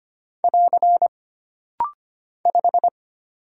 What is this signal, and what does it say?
Morse code, heard as a keyed tone, about 25 wpm, 700 Hz. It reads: <AR> 5
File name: Single Character (Letter:Number:Punctuation:Prosign) - Rapid-Fire - 25wpm